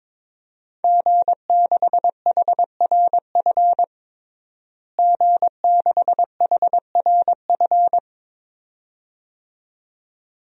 Morse code, heard as a keyed tone, about 21 wpm, 700 Hz. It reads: G6HRF G6HRF
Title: DXCC-100 Call Signs - 5 Characters - Repeated 2x - 22wpm